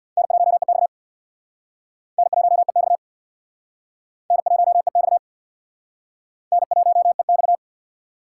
Morse code, heard as a keyed tone, about 63 wpm, 700 Hz. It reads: N1EX N1EX N1EX N1EX